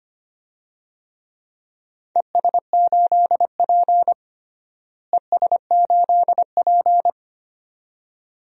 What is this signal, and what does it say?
Morse code, heard as a keyed tone, about 25 wpm, 700 Hz. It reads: ES8P ES8P
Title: DXCC-100 Call Signs - 4 Characters - Repeated 2x - 25wpm